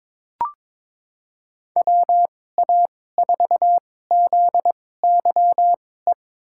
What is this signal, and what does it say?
Morse code, heard as a keyed tone, about 22 wpm, spaced as for 16 wpm, 700 Hz. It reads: WA4ZYE
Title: USA Call Signs - 2x Character Spacing - 22wpm